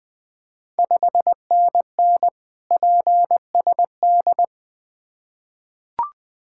Morse code, heard as a keyed tone, about 20 wpm, 700 Hz. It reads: 5NN PSD
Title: Ontario QSO Party Exchange 20wpm